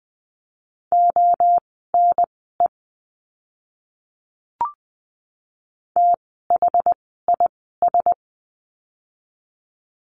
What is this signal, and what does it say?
Morse code, heard as a keyed tone, about 20 wpm, spaced as for 14 wpm, 700 Hz. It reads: ONE THIS
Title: 185 Lesson 30f - Letter b - Words 20wpm